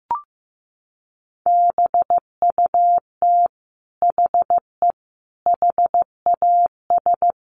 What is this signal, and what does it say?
Morse code, heard as a keyed tone, about 15 wpm, 700 Hz. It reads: BUT HE HAS
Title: Sentences from Top 600 Words 15wpm